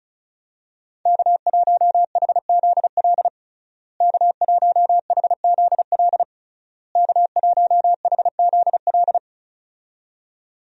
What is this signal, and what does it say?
Morse code, heard as a keyed tone, about 36 wpm, 700 Hz. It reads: K1HZL K1HZL K1HZL
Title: USA Call Signs - 5 Characters - Repeated 3x - 35wpm